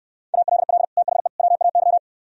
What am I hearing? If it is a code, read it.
Morse code, heard as a keyed tone, about 69 wpm, 700 Hz. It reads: ALL THE WAY